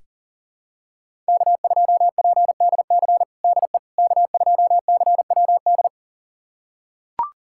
Morse code, heard as a keyed tone, about 40 wpm, 700 Hz. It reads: K2PDC DE K2CWD
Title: USA Callsign DE Callsign - 5 Characters - 40wpm